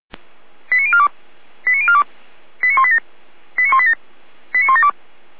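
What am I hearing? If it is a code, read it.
Five-tone selective call (ZVEI), heard as quick runs of five tones, tones of about 70 ms. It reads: 89042 89042 89187 89187 89172
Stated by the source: ZVEI five-tone selective call 89265 received by SDR.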